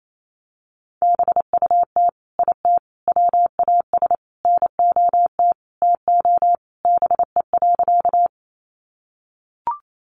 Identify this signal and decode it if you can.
Morse code, heard as a keyed tone, about 28 wpm, 700 Hz. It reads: BUT IT WAS NOT TO BE.
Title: Sentences from Top 600 Words 28wpm